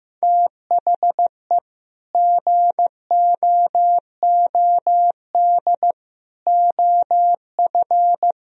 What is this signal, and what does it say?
Morse code, heard as a keyed tone, about 15 wpm, 700 Hz. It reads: THE GOOD OF
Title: Sentences from Top 1000 Words - Review Part 4 - 15wpm